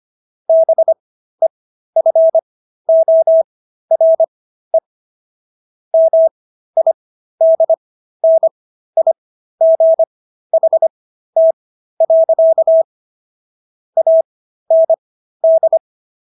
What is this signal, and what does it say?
Morse code, heard as a keyed tone, about 25 wpm, spaced as for 13 wpm, 650 Hz. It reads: BEFORE MIDNIGHT. AND
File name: SS106 13